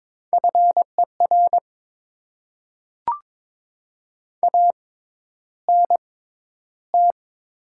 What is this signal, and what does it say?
Morse code, heard as a keyed tone, about 22 wpm, spaced as for 11 wpm, 700 Hz. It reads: FER ANT
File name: Top 20 QSO Elements - 6x Character Spacing - 22wpm